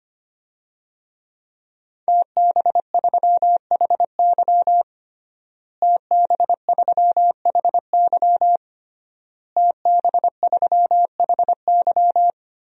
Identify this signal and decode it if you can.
Morse code, heard as a keyed tone, about 25 wpm, 700 Hz. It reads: TB3HY TB3HY TB3HY
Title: DXCC-100 Call Signs - 5 Characters - Repeated 3x - 25wpm